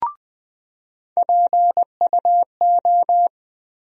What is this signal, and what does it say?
Morse code, heard as a keyed tone, about 21 wpm, 700 Hz. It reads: PUO
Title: USA Call Sign Postfixes - 3 Letters 20wpm